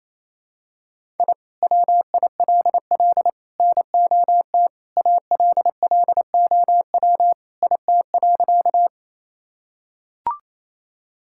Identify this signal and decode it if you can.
Morse code, heard as a keyed tone, about 28 wpm, 700 Hz. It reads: I WILL NOT ALLOW IT.
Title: Sentences from Top 1000 Words - Review Part 5 - 28wpm